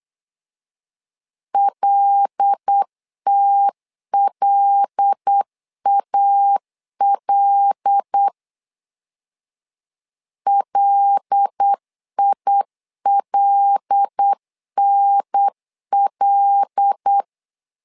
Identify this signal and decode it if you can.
Morse code, heard as a keyed tone, about 8 wpm, 800 Hz. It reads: LTLAL LILNL
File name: LEZIONI / 5 LBVC / 5 L 8wpm 2831